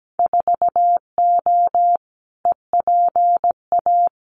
Morse code, heard as a keyed tone, about 17 wpm, 700 Hz. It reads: 4O EPA
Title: Winter Field Day Exchanges 17wpm